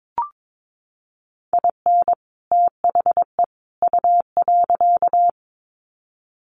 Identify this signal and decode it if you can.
Morse code, heard as a keyed tone, about 22 wpm, 700 Hz. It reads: IN THE U.
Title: Sets of 3 Words 22wpm